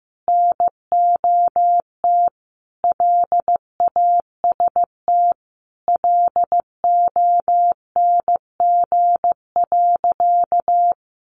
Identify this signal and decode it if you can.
Morse code, heard as a keyed tone, about 15 wpm, 700 Hz. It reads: NOT LAST LONG.
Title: Sentences from Top 1300 Words 15wpm